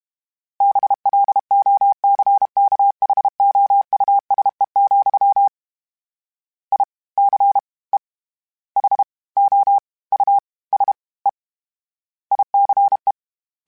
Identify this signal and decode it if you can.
Morse code, heard as a keyed tone, about 32 wpm, 800 Hz. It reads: BLOCKHOUSE, I C E H O U S E ICE
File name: General / General-mp3s / Houses @30